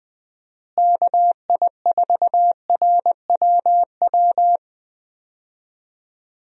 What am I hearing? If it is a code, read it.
Morse code, heard as a keyed tone, about 20 wpm, 700 Hz. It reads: KI4RWW
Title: USA Call Signs - 6 Characters - Repeated 2x - 20wpm